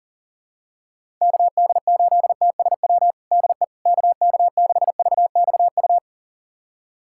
Morse code, heard as a keyed tone, about 40 wpm, 700 Hz. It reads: KD8TSW DE KK6VXU
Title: USA Callsign DE Callsign - 6 Characters - 40wpm